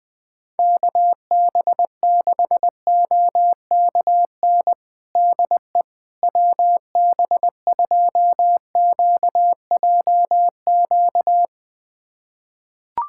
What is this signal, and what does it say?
Morse code, heard as a keyed tone, about 20 wpm, 700 Hz. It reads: KB6OKN DE WB2QJQ
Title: USA Callsign DE Callsign - 6 Characters - 20wpm